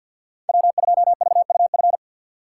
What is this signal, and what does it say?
Morse code, heard as a keyed tone, about 51 wpm, 700 Hz. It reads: W2VUF
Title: K1USN SST QSO Practice - 50wpm